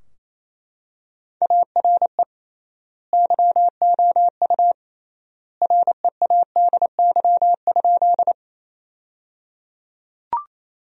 Morse code, heard as a keyed tone, about 28 wpm, 700 Hz. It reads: ARE YOU READY?
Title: 3x Word-Spacing - Sentences from Top 400 Words - 28wpm